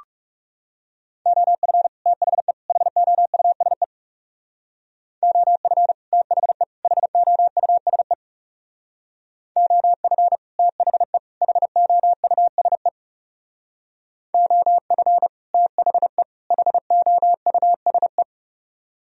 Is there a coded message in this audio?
Yes: Morse code from 1.3 to 18.2 s, heard as a keyed tone, about 39 wpm, 700 Hz. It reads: OF THE HOUSE OF THE HOUSE OF THE HOUSE OF THE HOUSE